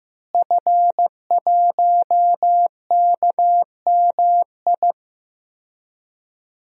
Morse code, heard as a keyed tone, about 15 wpm, 700 Hz. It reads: F1KMI
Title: DXCC-100 Call Signs - 5 Characters - Repeated 2x - 15wpm